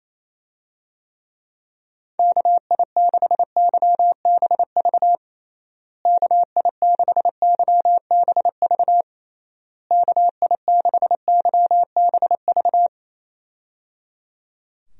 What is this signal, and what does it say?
Morse code, heard as a keyed tone, about 28 wpm, 700 Hz. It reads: KI6YBV KI6YBV KI6YBV
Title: USA Call Signs - 6 Characters - Repeated 3x - 28wpm